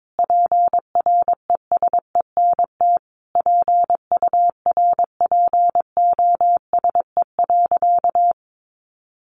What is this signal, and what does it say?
Morse code, heard as a keyed tone, about 22 wpm, 700 Hz. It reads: PRESENT PURPOSE.
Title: Sentences from Top 1400 Words 22wpm